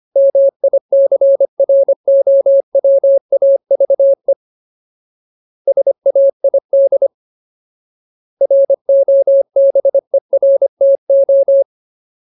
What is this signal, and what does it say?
Morse code, heard as a keyed tone, about 25 wpm, 550 Hz. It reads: MICROWAVE SAID ROBERTO
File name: Ultimate Word List / 25WPM Ultimate